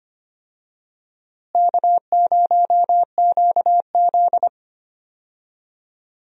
Morse code, heard as a keyed tone, about 25 wpm, 700 Hz. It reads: K0QZ
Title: USA Call Signs - 4 Characters - Repeated 3x - 25wpm